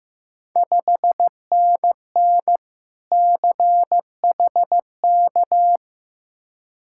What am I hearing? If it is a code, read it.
Morse code, heard as a keyed tone, about 15 wpm, 700 Hz. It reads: 5NN CHK
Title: Ontario QSO Party Exchange 15wpm